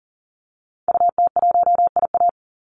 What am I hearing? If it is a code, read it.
Morse code, heard as a keyed tone, about 40 wpm, 700 Hz. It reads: UT1IA
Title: Call Signs - 40wpm